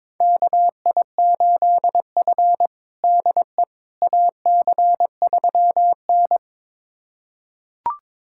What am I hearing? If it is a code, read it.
Morse code, heard as a keyed tone, about 22 wpm, 700 Hz. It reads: KI8F DE AC3N
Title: USA Callsign DE Callsign - 4 Characters - 22wpm